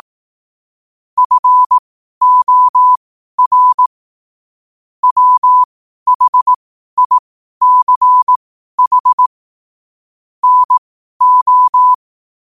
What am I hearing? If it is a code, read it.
Morse code, heard as a keyed tone, about 18 wpm, spaced as for 12 wpm, 1000 Hz. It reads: FOR WHICH NO
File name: qotdfiles / 12wpm